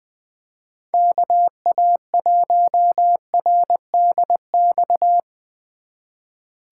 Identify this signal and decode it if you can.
Morse code, heard as a keyed tone, about 24 wpm, 700 Hz. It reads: KA1RDX
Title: Speed Racing - USA 2x3 Call Signs 20wpm